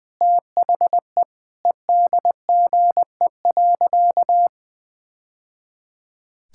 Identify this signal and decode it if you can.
Morse code, heard as a keyed tone, about 20 wpm, 700 Hz. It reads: THE EDGE.
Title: Sentences with 3-Word Phrases - Spoken Beginning - 20wpm